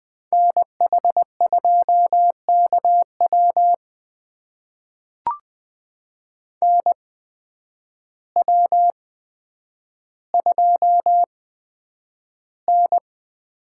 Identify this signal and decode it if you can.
Morse code, heard as a keyed tone, about 20 wpm, spaced as for 9 wpm, 700 Hz. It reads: NH2KW NW2N
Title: USA Call Signs - 8x Character Spacing - 20wpm